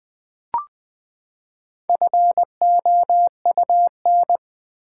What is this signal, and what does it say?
Morse code, heard as a keyed tone, about 20 wpm, 700 Hz. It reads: FOUN
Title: Build A Word - 4 Letters At A Time - 10-Letter Words - 20wpm